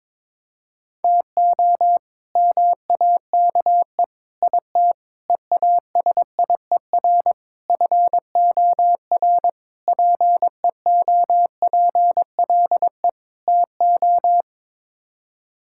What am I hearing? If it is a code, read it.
Morse code, heard as a keyed tone, about 22 wpm, 700 Hz. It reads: TO MAKE IT EASIER FOR PEOPLE TO
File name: practice / 22wpm / Sets of 7 Words - 22wpm